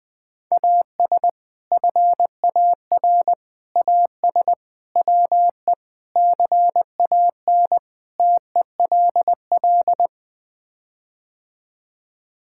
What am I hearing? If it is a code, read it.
Morse code, heard as a keyed tone, about 20 wpm, 700 Hz. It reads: AS FAR AS WE CAN TELL
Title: Sets of 6 Words - 20wpm